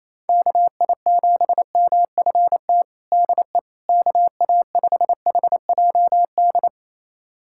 Morse code, heard as a keyed tone, about 27 wpm, 700 Hz. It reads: KI7MFT DE KA5HJD
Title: USA Callsign DE Callsign - 6 Characters - 28wpm